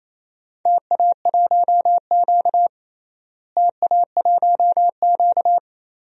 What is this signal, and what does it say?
Morse code, heard as a keyed tone, about 28 wpm, 700 Hz. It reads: TA1Q TA1Q
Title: DXCC-100 Call Signs - 4 Characters - Repeated 2x - 28wpm